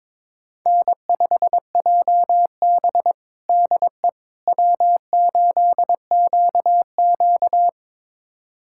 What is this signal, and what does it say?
Morse code, heard as a keyed tone, about 23 wpm, 700 Hz. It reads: N5JB DE W8QQ
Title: USA Callsign DE Callsign - 4 Characters - 22wpm